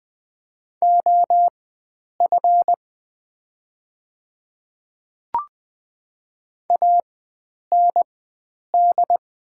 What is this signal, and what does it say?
Morse code, heard as a keyed tone, about 20 wpm, spaced as for 9 wpm, 690 Hz. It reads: OF AND
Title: 4x Character Spacing - Top 100 Words - 20wpm